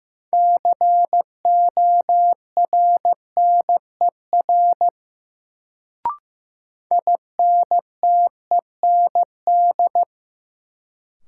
Morse code, heard as a keyed tone, about 15 wpm, 700 Hz. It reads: CORNER INTEND
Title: Sentences from Top 1600 Words - New Word Drill Alt - 15wpm